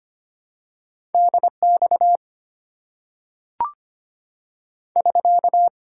Morse code, heard as a keyed tone, about 25 wpm, 700 Hz. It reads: DX <SK>
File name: Top 90 QSO Elements - 6x Character Spacing - 25wpm